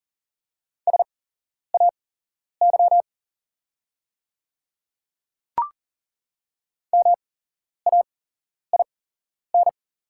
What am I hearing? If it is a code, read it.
Morse code, heard as a keyed tone, about 41 wpm, spaced as for 11 wpm, 690 Hz. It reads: SAY MAIN
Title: 8x Character Spacing - Top 500 Words - 40wpm